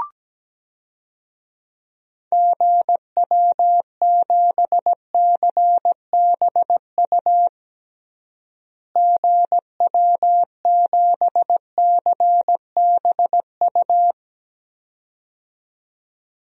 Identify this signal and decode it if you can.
Morse code, heard as a keyed tone, about 17 wpm, 700 Hz. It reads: GW7CBU GW7CBU
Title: DXCC-100 Call Signs - 6 Characters - Repeated 2x - 17wpm